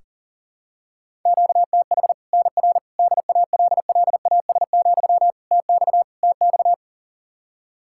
Morse code, heard as a keyed tone, about 40 wpm, 700 Hz. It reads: QTH NR DALLAS, TX TX
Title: Practice QSO - Exchange 1 - Beginning - 40wpm